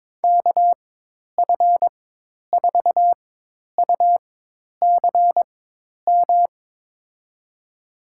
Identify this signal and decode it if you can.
Morse code, heard as a keyed tone, about 22 wpm, spaced as for 10 wpm, 700 Hz. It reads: KF4UCM
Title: USA Call Signs - 6 Characters - 4x Character Spacing - 22wpm